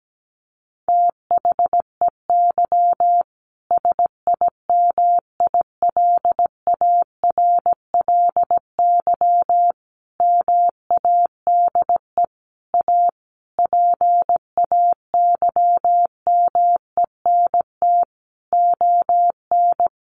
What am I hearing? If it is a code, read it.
Morse code, heard as a keyed tone, about 17 wpm, 700 Hz. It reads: THEY SIMILARLY MADE A PAYMENT ON
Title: Sentences from Top 1400 Words 17wpm